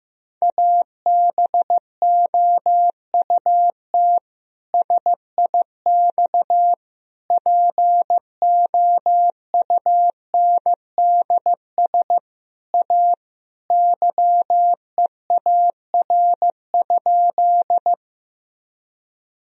Morse code, heard as a keyed tone, about 16 wpm, 700 Hz. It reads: ABOUT SIX POUNDS A YEAR?
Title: Sentences from Top 1000 Words - Review Part 2 - 15wpm